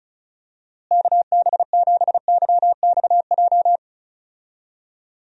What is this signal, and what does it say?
Morse code, heard as a keyed tone, about 34 wpm, 700 Hz. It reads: KB7YXJ
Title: USA Call Signs - 6 Characters - Repeated 2x - 35wpm